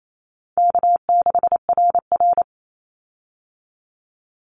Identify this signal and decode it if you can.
Morse code, heard as a keyed tone, about 28 wpm, 700 Hz. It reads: K6RR
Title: Call Signs Repeated 2x - 28wpm